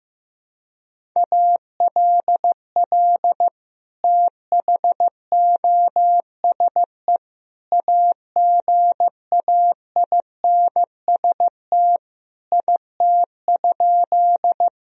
Morse code, heard as a keyed tone, about 15 wpm, 700 Hz. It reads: ALL THOSE AGAINST IT?
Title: Sentences from Top 1000 Words - Review Part 1 - 15wpm